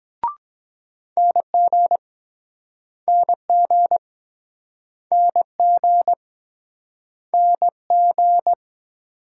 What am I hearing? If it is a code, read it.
Morse code, heard as a keyed tone, about 23 wpm, 700 Hz. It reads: NG NG NG NG